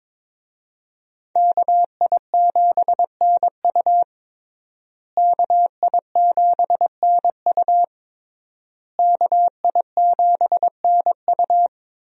Morse code, heard as a keyed tone, about 22 wpm, 700 Hz. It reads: KI7NU KI7NU KI7NU
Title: USA Call Signs - 5 Characters - Repeated 3x - 22wpm